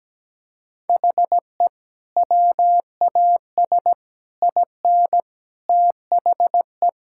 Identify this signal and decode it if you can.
Morse code, heard as a keyed tone, about 17 wpm, 700 Hz. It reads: HE WAS IN THE
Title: Sentences from Top 1600 Words 17wpm